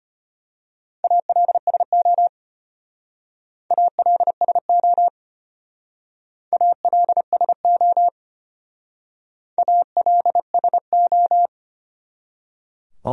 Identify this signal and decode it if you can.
Morse code, heard as a keyed tone, about 30 wpm, 700 Hz. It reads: ALSO ALSO ALSO ALSO
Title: Speed-Racing Top 500 Words 25wpm